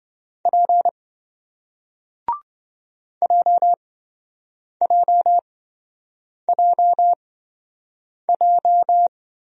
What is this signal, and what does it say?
Morse code, heard as a keyed tone, about 28 wpm, spaced as for 7 wpm, 700 Hz. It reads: P JJJJ